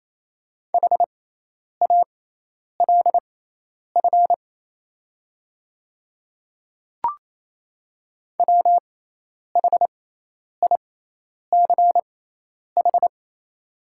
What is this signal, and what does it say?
Morse code, heard as a keyed tone, about 28 wpm, spaced as for 10 wpm, 700 Hz. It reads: HALF WHICH
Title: 6x Character Spacing - Top 500 Words - 28wpm